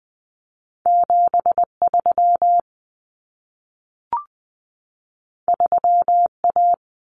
Morse code, heard as a keyed tone, about 20 wpm, 700 Hz. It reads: 73 3A
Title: Common Cut Numbers - 2-Digit 20wpm